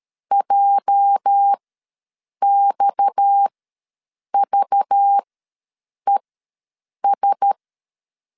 Morse code, heard as a keyed tone, about 13 wpm, spaced as for 7 wpm, 800 Hz. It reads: JXVES
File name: LEZIONI / 8 YJ45 / 8 YJ4 ws 12wpm 3249